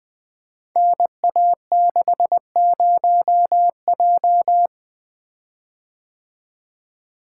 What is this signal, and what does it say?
Morse code, heard as a keyed tone, about 20 wpm, 700 Hz. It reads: NA60J
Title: Code Groups - 5 Letter:Numbers - 20wpm